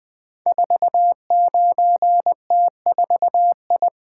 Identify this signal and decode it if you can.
Morse code, heard as a keyed tone, about 20 wpm, 700 Hz. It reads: 49T4I
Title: Code Groups - 5 Letter:Numbers - 20wpm